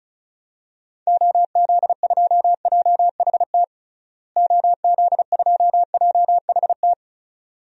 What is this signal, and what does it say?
Morse code, heard as a keyed tone, about 34 wpm, 700 Hz. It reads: OZ2JHT OZ2JHT
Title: DXCC-100 Call Signs - 6 Characters - Repeated 2x - 35wpm